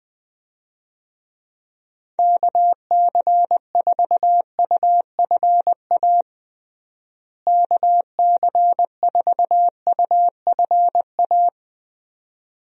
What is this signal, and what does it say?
Morse code, heard as a keyed tone, about 21 wpm, 700 Hz. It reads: KC4UFA KC4UFA
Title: USA Call Signs - 6 Characters - Repeated 2x - 20wpm